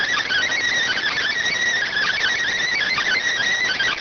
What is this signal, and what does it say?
Emergency Alert System (SAME) tones, heard as harsh buzzing bursts. It is heard from 0.8 to 3.3 s.